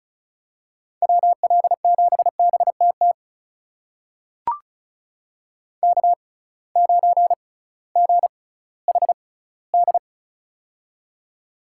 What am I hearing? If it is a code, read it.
Morse code, heard as a keyed tone, about 36 wpm, spaced as for 19 wpm, 700 Hz. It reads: WL7BTT K9GHD
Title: USA Call Signs - 6x Character Spacing - 35wpm